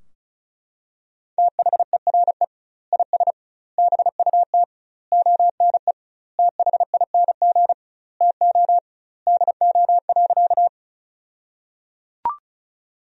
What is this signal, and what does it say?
Morse code, heard as a keyed tone, about 34 wpm, 700 Hz. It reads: THERE IS BUT ONE THING TO DO.
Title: 2x Word-Spacing - Sentences from Top 100 Words - 35wpm